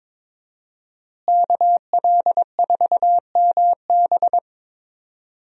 Morse code, heard as a keyed tone, about 22 wpm, 700 Hz. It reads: KL4MB
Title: USA Call Signs - 5 Characters - Repeated 2x - 22wpm